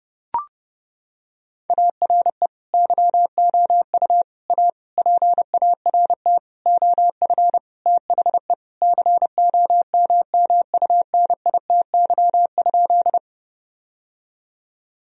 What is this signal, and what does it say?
Morse code, heard as a keyed tone, about 30 wpm, 700 Hz. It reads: ARE YOU A PART OF THE COMMUNITY?
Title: Sentences from Top 1000 Words - Review Part 4 - 30wpm